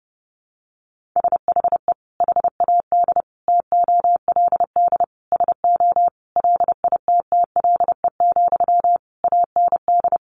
Morse code, heard as a keyed tone, about 30 wpm, 700 Hz. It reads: SHE HAD TOLD SO LITTLE, AND
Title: Sentences from Top 1500 Words 30wpm